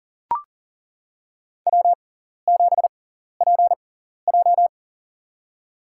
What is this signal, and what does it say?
Morse code, heard as a keyed tone, about 40 wpm, spaced as for 14 wpm, 700 Hz. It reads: W7PJ
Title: USA Call Signs - 4 Characters - 6x Character Spacing - 40wpm